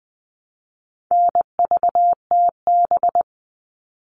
Morse code, heard as a keyed tone, about 20 wpm, 700 Hz. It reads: NVTB
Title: All Cut Numbers - 4-Digit 20wpm